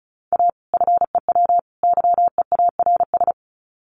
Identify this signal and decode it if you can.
Morse code, heard as a keyed tone, about 34 wpm, 700 Hz. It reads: A FEW YEARS
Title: Sets of 3 Words 35wpm